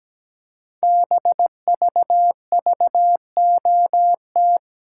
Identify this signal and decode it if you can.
Morse code, heard as a keyed tone, about 17 wpm, 700 Hz. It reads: BVVOT
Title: Code Groups - 5 Letter:Numbers - 17wpm